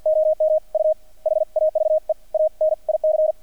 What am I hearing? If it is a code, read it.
Morse code, heard as a keyed tone, about 50 wpm, spaced as for 44 wpm, 620 Hz. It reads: OM U HAVE A NIC
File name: archived / CRSnet